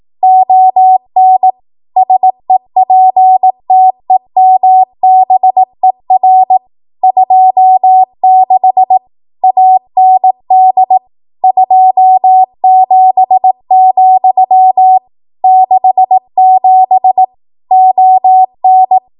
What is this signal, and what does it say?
Morse code, heard as a keyed tone, about 18 wpm, 750 Hz. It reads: ON SEPTEMBER 26 AND 27, 67 ON
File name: Archive / 18 WPM / 18WPM